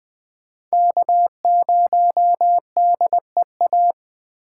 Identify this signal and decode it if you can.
Morse code, heard as a keyed tone, about 20 wpm, 700 Hz. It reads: K0DEA
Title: USA Call Signs - 5 Characters - 4x Character Spacing - 20wpm